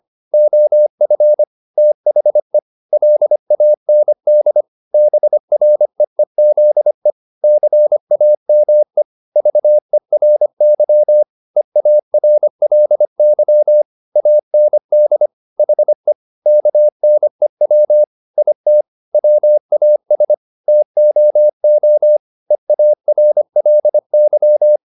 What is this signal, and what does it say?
Morse code, heard as a keyed tone, about 25 wpm, 600 Hz. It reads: OF THE LAND BREEZE CAME VERY EARLY AND HE KNEW IT WAS TOO EARLY
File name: the old man and the sea part 1 25wpm